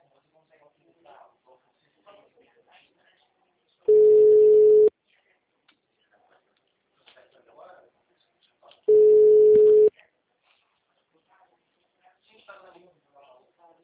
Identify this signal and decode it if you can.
Ringback tone (Europe, 425 Hz), with speech.